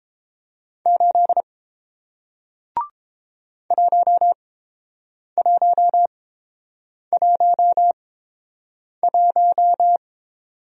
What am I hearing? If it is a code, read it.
Morse code, heard as a keyed tone, about 30 wpm, spaced as for 7 wpm, 700 Hz. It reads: 8 1111